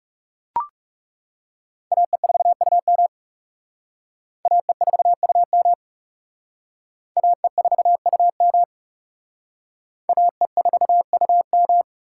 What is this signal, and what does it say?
Morse code, heard as a keyed tone, about 36 wpm, 700 Hz. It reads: AE4UM AE4UM AE4UM AE4UM
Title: Speed Racing - USA 2x2 Call Signs 30wpm